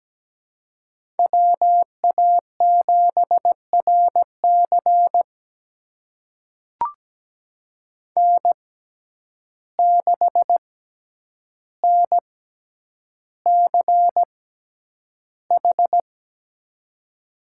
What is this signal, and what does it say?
Morse code, heard as a keyed tone, about 17 wpm, spaced as for 9 wpm, 700 Hz. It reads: WA7RC N6NCH
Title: USA Call Signs - 5 Characters - 6x Character Spacing - 17wpm